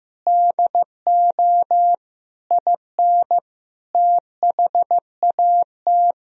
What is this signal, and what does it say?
Morse code, heard as a keyed tone, about 15 wpm, 700 Hz. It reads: DO IN THAT
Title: Sentences from Top 1000 Words - Review Part 3 - 15wpm